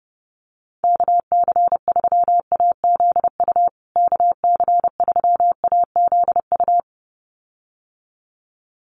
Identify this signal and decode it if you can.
Morse code, heard as a keyed tone, about 30 wpm, 700 Hz. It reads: KC3AZU KC3AZU
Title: USA Call Signs Repeated 2x 30wpm